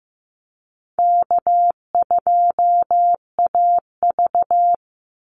Morse code, heard as a keyed tone, about 15 wpm, 700 Hz. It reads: K2AV
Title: Easy Call Signs Repeated 3x - 15wpm